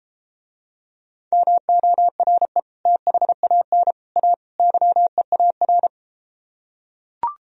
Morse code, heard as a keyed tone, about 34 wpm, 700 Hz. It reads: MORE THAN A YEAR